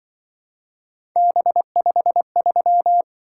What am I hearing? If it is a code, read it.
Morse code, heard as a keyed tone, about 23 wpm, 700 Hz. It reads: B53
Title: Speed-Racing - Three Letter-Number Combinations - 20wpm